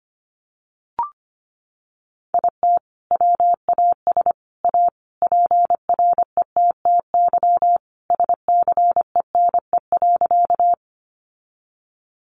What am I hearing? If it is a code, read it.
Morse code, heard as a keyed tone, about 25 wpm, 700 Hz. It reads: IT WAS A PRETTY SCENE.
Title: Sentences from Top 1100 Words 25wpm